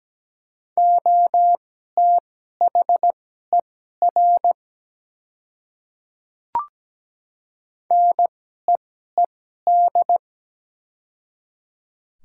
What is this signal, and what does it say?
Morse code, heard as a keyed tone, about 17 wpm, spaced as for 12 wpm, 700 Hz. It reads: OTHER NEED
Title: 2x Character Spacing - Top 200 Words - 17wpm